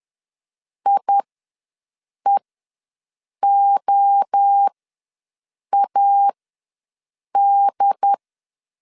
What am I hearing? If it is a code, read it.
Morse code, heard as a keyed tone, about 10 wpm, spaced as for 6 wpm, 800 Hz. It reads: IEOAD
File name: LEZIONI / 5 LBVC / 5 L ws 10wpm 3636